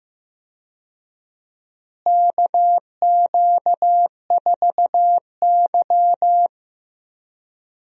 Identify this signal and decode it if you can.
Morse code, heard as a keyed tone, about 15 wpm, 700 Hz. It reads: KQ4Y
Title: USA Call Signs - 4 Characters - Repeated 2x - 15wpm